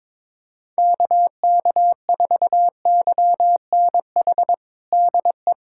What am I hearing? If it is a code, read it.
Morse code, heard as a keyed tone, about 22 wpm, 700 Hz. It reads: KK4YNH DE
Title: USA Callsign DE Callsign - 6 Characters - 22wpm